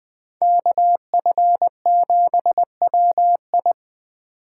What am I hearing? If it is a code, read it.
Morse code, heard as a keyed tone, about 21 wpm, 700 Hz. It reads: KF7WI
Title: 2023 SOTA Chaser Call Signs - 20wpm